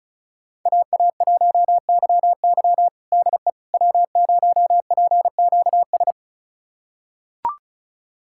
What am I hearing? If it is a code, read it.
Morse code, heard as a keyed tone, about 34 wpm, 690 Hz. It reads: AA1YY DE W0PQS
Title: USA Callsign DE Callsign - 5 Characters - 35wpm